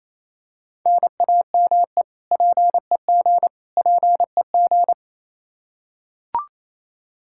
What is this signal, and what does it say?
Morse code, heard as a keyed tone, about 28 wpm, 700 Hz. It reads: NAME PEG PEG
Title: Practice QSO - Exchange 1 - Beginning - 28wpm